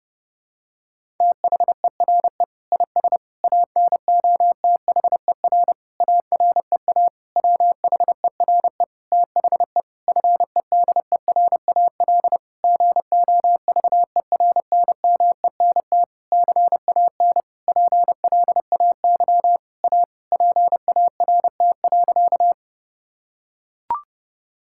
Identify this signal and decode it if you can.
Morse code, heard as a keyed tone, about 32 wpm, 700 Hz. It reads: THERE IS ANOTHER AREA WHERE THE FEDERAL GOVERNMENT CAN PLAY A PART.
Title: Sentences from Top 1000 Words - Review Part 5 - 30wpm